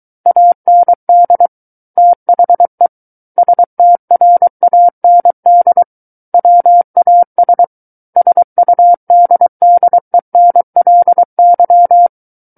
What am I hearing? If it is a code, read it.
Morse code, heard as a keyed tone, about 23 wpm, 700 Hz. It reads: AND THE STRAND WAS SUDDENLY
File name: Book One / 14-Chapter 14 - 23 WPM